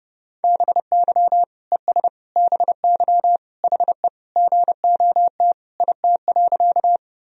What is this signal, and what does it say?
Morse code, heard as a keyed tone, about 30 wpm, 700 Hz. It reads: BY ES BY HE GOT IT.
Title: Sentences from Top 300 Words and Abbreviations 30wpm